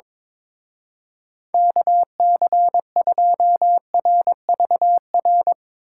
Morse code, heard as a keyed tone, about 24 wpm, 700 Hz. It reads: KC2RVR
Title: Speed Racing - USA Call Signs 22wpm